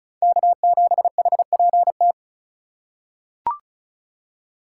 Morse code, heard as a keyed tone, about 34 wpm, 700 Hz. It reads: K7HPT
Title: K1USN SST QSO Practice - 35wpm